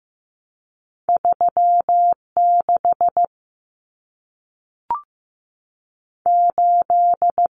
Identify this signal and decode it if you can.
Morse code, heard as a keyed tone, about 15 wpm, 700 Hz. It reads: 36 8
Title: Common Cut Numbers - 2-Digit 15wpm